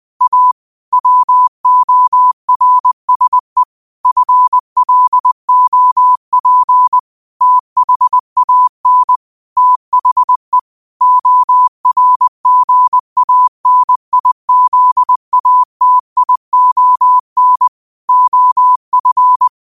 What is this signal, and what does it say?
Morse code, heard as a keyed tone, about 20 wpm, 1000 Hz. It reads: A WORSE FLOP THAN THE ORGANIZATION OF
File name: qotdfiles / 20wpm